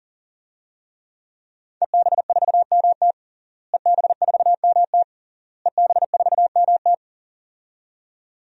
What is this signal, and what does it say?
Morse code, heard as a keyed tone, about 37 wpm, 700 Hz. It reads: EB4MT EB4MT EB4MT
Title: DXCC-100 Call Signs - 5 Characters - Repeated 3x - 40wpm